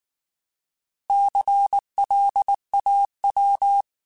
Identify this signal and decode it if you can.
Morse code, heard as a keyed tone, about 19 wpm, 800 Hz. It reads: CLAW
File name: Alphabetical / Alphabetical mp3s / Cs @18